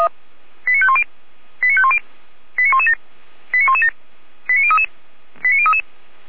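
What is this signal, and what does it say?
Five-tone selective call (ZVEI), heard as quick runs of five tones, tones of about 70 ms. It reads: 89510 89510 89117 89117 89033 89033
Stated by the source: ZVEI five-tone selective call received on SDR.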